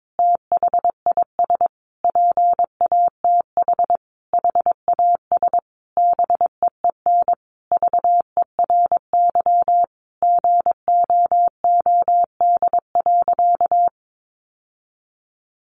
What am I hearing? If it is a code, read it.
Morse code, heard as a keyed tone, about 22 wpm, 700 Hz. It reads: THIS PATH HAS BEEN VERY GOOD.
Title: Sentences from Top 1400 Words 22wpm